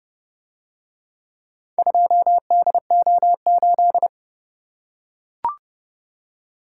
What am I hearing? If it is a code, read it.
Morse code, heard as a keyed tone, about 28 wpm, 700 Hz. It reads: 2DO8